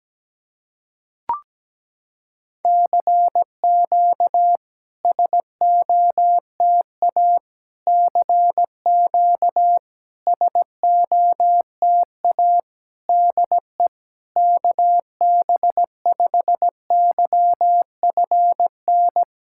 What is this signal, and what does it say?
Morse code, heard as a keyed tone, about 17 wpm, 690 Hz. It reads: CQ SOTA CQ SOTA DE KB5YFN
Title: SOTA QSO Practice 17wpm